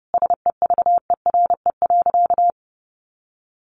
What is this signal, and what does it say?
Morse code, heard as a keyed tone, about 30 wpm, 700 Hz. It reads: SEVERE.
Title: Sentences from Top 1400 Words 30wpm